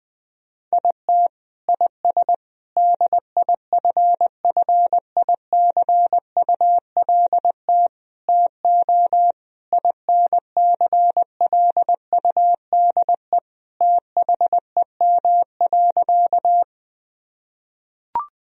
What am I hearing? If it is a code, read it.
Morse code, heard as a keyed tone, about 20 wpm, 700 Hz. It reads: IT IS DIFFICULT TO INCLUDE THEM.
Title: Sentences from Top 1000 Words - Review Part 3 - 20wpm